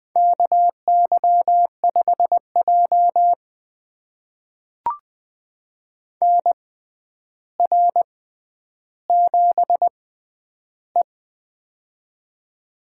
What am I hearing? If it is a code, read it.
Morse code, heard as a keyed tone, about 20 wpm, spaced as for 10 wpm, 700 Hz. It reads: KY5J NR7E
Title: USA Call Signs - 4 Characters - 6x Character Spacing - 20wpm